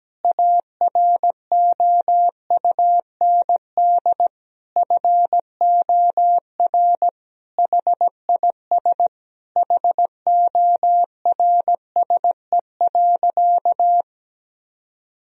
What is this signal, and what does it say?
Morse code, heard as a keyed tone, about 17 wpm, 700 Hz. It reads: AROUND FOR HIS HORSE.
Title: Sentences from Top 1000 Words - Review Part 3 - 17wpm